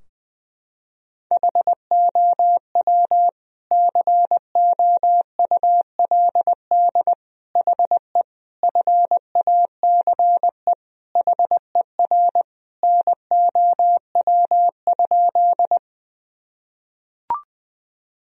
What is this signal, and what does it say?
Morse code, heard as a keyed tone, about 20 wpm, 700 Hz. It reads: HOW COULD HE FACE HER NOW?
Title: Sentences from Top 1000 Words - Review Part 2 - 20wpm